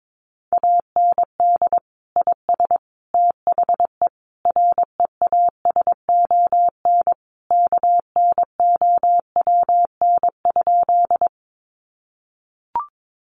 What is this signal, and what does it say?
Morse code, heard as a keyed tone, about 22 wpm, 700 Hz. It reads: AND IS THE REASON KNOWN?
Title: Sentences from Top 700 Words 22wpm